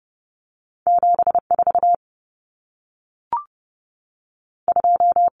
Morse code, heard as a keyed tone, about 30 wpm, 700 Hz. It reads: 7 4 2
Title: Numbers - 2-Digit 30wpm